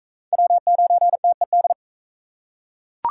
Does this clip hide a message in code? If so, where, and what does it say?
Yes: Morse code from 0.3 to 1.7 s, heard as a keyed tone, about 43 wpm, 700 Hz. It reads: W9TED